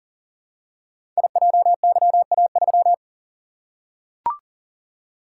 Morse code, heard as a keyed tone, about 42 wpm, 700 Hz. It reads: IJYA3
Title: Code Groups - 5 Letter:Numbers - 40wpm